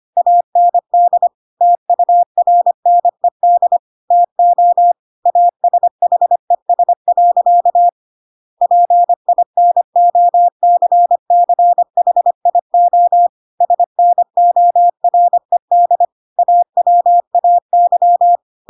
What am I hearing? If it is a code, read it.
Morse code, heard as a keyed tone, about 25 wpm, 700 Hz. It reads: AND TURNED TO ASHES. PINOCCHIO SNORED AWAY